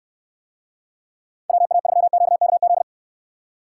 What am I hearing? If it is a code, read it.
Morse code, heard as a keyed tone, about 67 wpm, 700 Hz. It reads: WA3QKZ